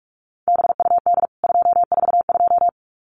Morse code, heard as a keyed tone, about 44 wpm, 700 Hz. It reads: BUD 242
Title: CWT Exchange 45wpm